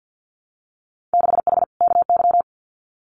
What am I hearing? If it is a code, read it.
Morse code, heard as a keyed tone, about 50 wpm, 700 Hz. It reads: 6H KY
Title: Winter Field Day Exchanges Encore 50wpm